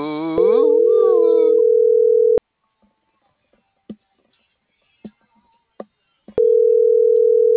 Ringback tone (North America), 440 + 480 Hz.